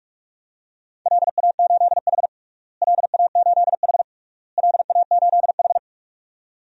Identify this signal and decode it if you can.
Morse code, heard as a keyed tone, about 42 wpm, 700 Hz. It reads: LA8H LA8H LA8H
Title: DXCC-100 Call Signs - 4 Characters - Repeated 3x - 45wpm